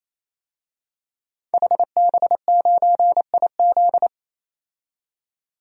Morse code, heard as a keyed tone, about 27 wpm, 700 Hz. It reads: HB9IZ
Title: DXCC-100 Call Signs - 5 Characters - 28wpm